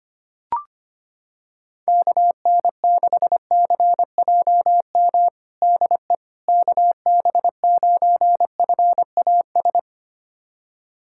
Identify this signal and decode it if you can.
Morse code, heard as a keyed tone, about 25 wpm, 700 Hz. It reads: KN6CJM DE KB9FAS
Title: USA Callsign DE Callsign 25wpm